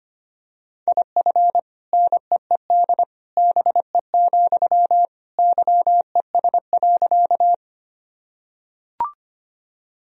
Morse code, heard as a keyed tone, about 25 wpm, 700 Hz. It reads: IF NEED BE, YES.
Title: Sentences from Top 1000 Words - Review Part 1 - 25wpm